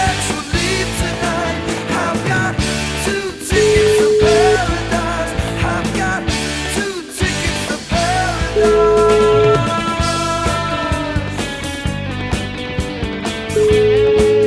Ringback tone (Europe, 425 Hz), with noise.